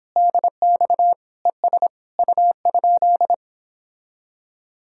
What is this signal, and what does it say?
Morse code, heard as a keyed tone, about 26 wpm, 700 Hz. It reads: DX ES U?
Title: Speed-Racing Common QSO Phrases Encore - 22wpm